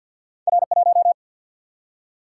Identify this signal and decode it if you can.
Morse code, heard as a keyed tone, about 52 wpm, 700 Hz. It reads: R 1
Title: Single-Letter Single-Number 50wpm